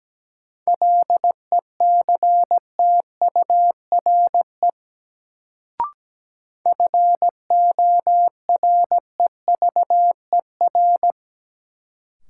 Morse code, heard as a keyed tone, about 17 wpm, 700 Hz. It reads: LECTURE FOREVER
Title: Sentences from Top 1600 Words - New Word Drill Alt - 17wpm